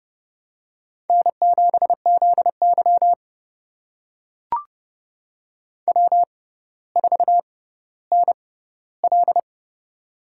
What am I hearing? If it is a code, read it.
Morse code, heard as a keyed tone, about 29 wpm, spaced as for 15 wpm, 700 Hz. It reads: N7ZY W4NL
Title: USA Call Signs - 4 Characters - 6x Character Spacing - 30wpm